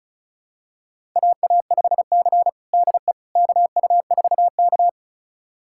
Morse code, heard as a keyed tone, about 36 wpm, 700 Hz. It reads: AA5C DE KU4K
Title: USA Callsign DE Callsign - 4 Characters - 35wpm